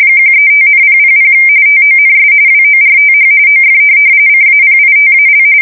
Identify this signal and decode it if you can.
RTTY, about 45.45 baud. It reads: 135,000 TONS WORTH OF CAPITAL
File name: fsk170 / USS-New-Jersey